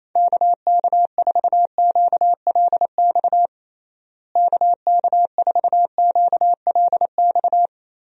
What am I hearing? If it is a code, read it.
Morse code, heard as a keyed tone, about 28 wpm, 700 Hz. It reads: KK4QLX KK4QLX
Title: USA Call Signs - 6 Characters - Repeated 3x - 28wpm